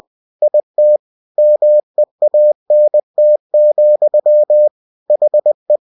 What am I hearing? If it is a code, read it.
Morse code, heard as a keyed tone, about 20 wpm, 600 Hz. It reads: IT MEANT, HE
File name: cw-books / winnie-the-pooh / winnie-20wpm-page-0020